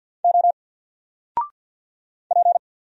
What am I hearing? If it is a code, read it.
Morse code, heard as a keyed tone, about 48 wpm, 700 Hz. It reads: O P
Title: Single Character (Letter:Number:Punctuation:Prosign) 50wpm